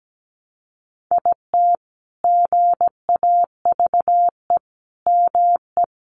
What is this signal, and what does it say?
Morse code, heard as a keyed tone, about 17 wpm, 700 Hz. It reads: IT GAVE ME
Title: Sentences from Top 1300 Words 17wpm